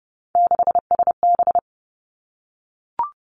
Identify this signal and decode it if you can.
Morse code, heard as a keyed tone, about 30 wpm, 700 Hz. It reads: 6SB
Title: Three Challenging Letter-Number Combinations 30wpm